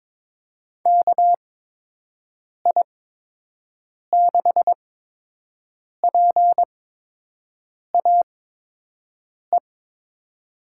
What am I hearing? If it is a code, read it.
Morse code, heard as a keyed tone, about 21 wpm, spaced as for 6 wpm, 700 Hz. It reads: KI6PAE
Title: USA Call Signs - 6 Characters - 8x Character Spacing - 22wpm